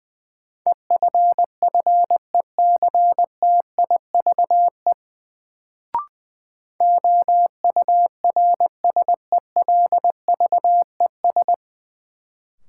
Morse code, heard as a keyed tone, about 20 wpm, 700 Hz. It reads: EFFECTIVE OURSELVES
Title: Sentences from Top 1600 Words - New Word Drill Alt - 20wpm